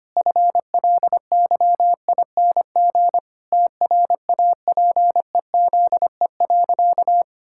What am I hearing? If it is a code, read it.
Morse code, heard as a keyed tone, about 25 wpm, 700 Hz. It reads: FLYING TRAPEZE.
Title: Pangram Challenge v2 - 25wpm